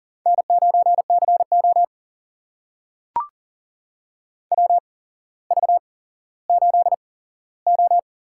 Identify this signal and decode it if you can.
Morse code, heard as a keyed tone, about 40 wpm, spaced as for 17 wpm, 700 Hz. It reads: N9CO WV8O
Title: USA Call Signs - 4 Characters - 8x Character Spacing - 40wpm